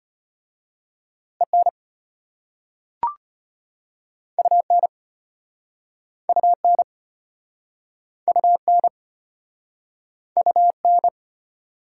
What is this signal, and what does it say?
Morse code, heard as a keyed tone, about 33 wpm, 700 Hz. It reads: EN UN UN UN UN